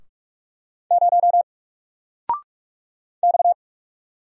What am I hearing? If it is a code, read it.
Morse code, heard as a keyed tone, about 46 wpm, 700 Hz. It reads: 0 X
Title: Single Character (Letter:Number:Punctuation:Prosign) 45wpm